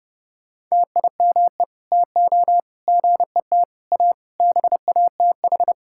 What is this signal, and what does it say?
Morse code, heard as a keyed tone, about 30 wpm, 700 Hz. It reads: TIME TO GET A BATH
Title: Sentences from Top 1600 Words 30wpm